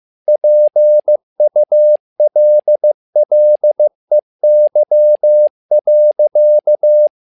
Morse code, heard as a keyed tone, about 15 wpm, 600 Hz. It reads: PULLEY.
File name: learning center / tower trouble / tower trouble 15wpm